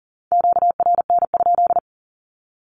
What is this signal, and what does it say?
Morse code, heard as a keyed tone, about 40 wpm, 700 Hz. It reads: QRN?
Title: Q-Codes 40wpm